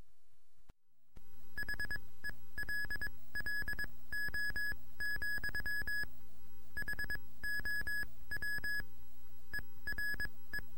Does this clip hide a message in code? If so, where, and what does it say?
Yes: Morse code from 1.6 to 10.6 s, heard as a keyed tone, about 22 wpm, spaced as for 17 wpm, 1700 Hz. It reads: HELLO, HOW ERE